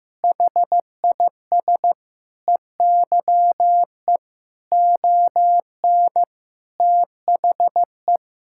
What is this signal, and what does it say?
Morse code, heard as a keyed tone, about 15 wpm, 700 Hz. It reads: HIS EYE ON THE
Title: Sentences from Top 1000 Words - Review Part 5 - 15wpm